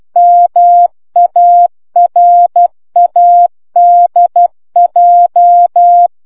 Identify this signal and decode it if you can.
Morse code, heard as a keyed tone, about 12 wpm, 700 Hz. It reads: MARADJ